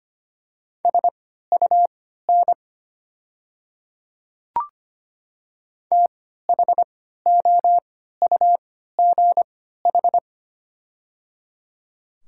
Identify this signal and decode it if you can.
Morse code, heard as a keyed tone, about 25 wpm, spaced as for 14 wpm, 700 Hz. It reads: SUN THOUGH
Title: 3x Character Spacing - Top 500 Words - 25wpm